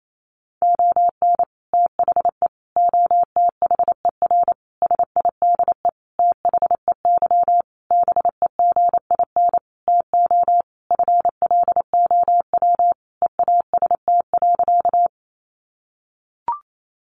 Morse code, heard as a keyed tone, about 28 wpm, 700 Hz. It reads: ON THE OTHER SIDE THEY BEGIN TO FLOW EAST.
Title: Sentences from Top 1200 Words 28wpm